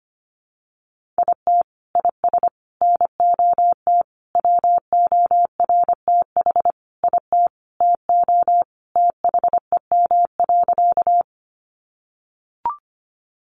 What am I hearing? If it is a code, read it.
Morse code, heard as a keyed tone, about 25 wpm, 700 Hz. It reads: IT IS NOT WORTH IT TO THEM.
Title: Sentences from Top 800 Words 25wpm